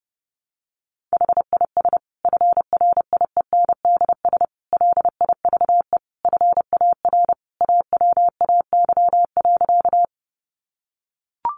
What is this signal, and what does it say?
Morse code, heard as a keyed tone, about 30 wpm, 700 Hz. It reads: HIS FRIENDS LIVE FAR AWAY.
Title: Additional Easy Phrases 30wpm